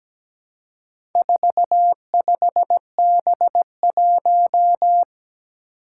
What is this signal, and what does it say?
Morse code, heard as a keyed tone, about 17 wpm, 700 Hz. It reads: 45B1
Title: Four Challenging Letter-Number Combinations 17wpm